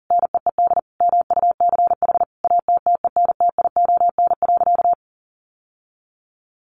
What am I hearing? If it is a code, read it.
Morse code, heard as a keyed tone, about 40 wpm, 700 Hz. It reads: NEED MUCH ATTENTION.
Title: Sentences from Top 1000 Words 40wpm